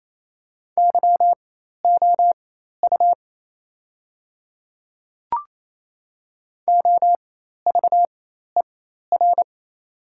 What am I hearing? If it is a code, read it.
Morse code, heard as a keyed tone, about 28 wpm, spaced as for 13 wpm, 690 Hz. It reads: YOU OVER
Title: 4x Character Spacing - Top 100 Words - 28wpm